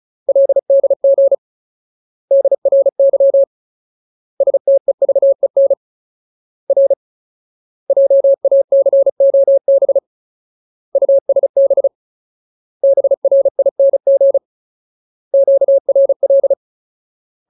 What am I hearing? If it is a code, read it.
Morse code, heard as a keyed tone, about 34 wpm, 550 Hz. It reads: LDG DRY STEVEN R JACOB USB BRING QRL
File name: Ultimate Word List / 35WPM Ultimate